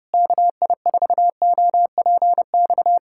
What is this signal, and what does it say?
Morse code, heard as a keyed tone, about 32 wpm, 700 Hz. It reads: KI4OPX
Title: USA Callsign DE Callsign 30wpm